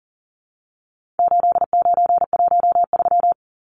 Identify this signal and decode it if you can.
Morse code, heard as a keyed tone, about 40 wpm, 700 Hz. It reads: 8913
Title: Numbers - 4-Digit 40wpm